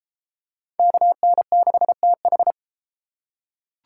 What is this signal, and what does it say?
Morse code, heard as a keyed tone, about 33 wpm, 700 Hz. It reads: KN6TH